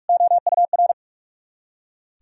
Morse code, heard as a keyed tone, about 55 wpm, 690 Hz. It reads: OUR
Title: Speed-Racing - Sets of 2 Words 45wpm